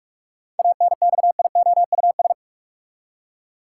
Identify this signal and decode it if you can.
Morse code, heard as a keyed tone, about 46 wpm, 700 Hz. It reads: ANXIOUS
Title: Seven Letter Words - 45wpm